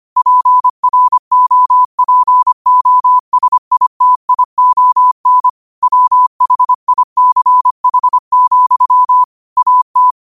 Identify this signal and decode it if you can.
Morse code, heard as a keyed tone, about 25 wpm, 1000 Hz. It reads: PROPOSITION WHICH, AT